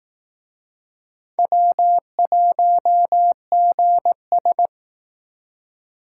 Morse code, heard as a keyed tone, about 18 wpm, 700 Hz. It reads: W1GS
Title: Speed Racing - USA 1x2 Call Signs 15wpm